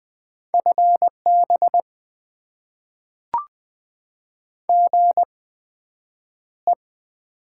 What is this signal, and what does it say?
Morse code, heard as a keyed tone, about 20 wpm, spaced as for 9 wpm, 700 Hz. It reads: FB GE
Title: Top 20 QSO Elements - 8x Character Spacing - 20wpm